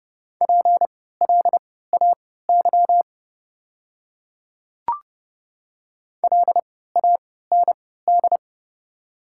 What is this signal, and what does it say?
Morse code, heard as a keyed tone, about 30 wpm, spaced as for 17 wpm, 700 Hz. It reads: PLAY LAND
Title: 3x Character Spacing - Top 300 Words - 30wpm